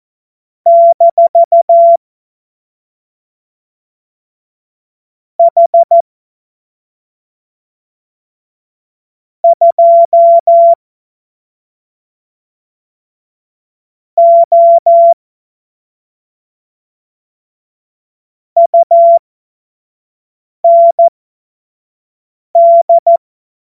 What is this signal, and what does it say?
Morse code, heard as a keyed tone, about 14 wpm, spaced as for 3 wpm, 680 Hz. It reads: - H2O UND